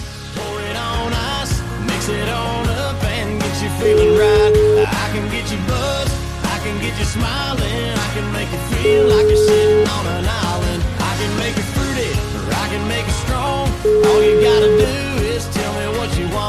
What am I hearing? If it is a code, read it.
Ringback tone (Europe, 425 Hz), with noise.